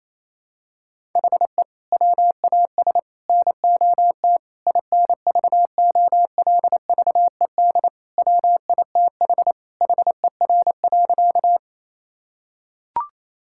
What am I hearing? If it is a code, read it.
Morse code, heard as a keyed tone, about 28 wpm, 700 Hz. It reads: HE WAS NOT INVOLVED WITH HER.
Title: Sentences from Top 1000 Words - Review Part 5 - 28wpm